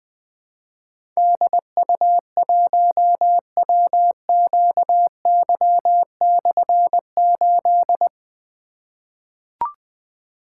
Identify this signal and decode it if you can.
Morse code, heard as a keyed tone, about 20 wpm, 700 Hz. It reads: DU1WQY/8
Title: Difficult Call Signs 20wpm